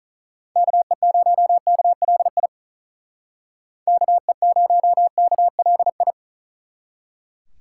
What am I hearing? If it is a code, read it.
Morse code, heard as a keyed tone, about 44 wpm, 700 Hz. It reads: KE0KLI KE0KLI
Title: Speed Racing - USA 2x3 Call Signs 35wpm